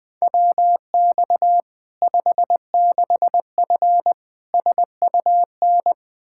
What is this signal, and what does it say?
Morse code, heard as a keyed tone, about 20 wpm, 700 Hz. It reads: WX 56F SUN
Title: Practice QSO - Exchange 2 - Info - 20wpm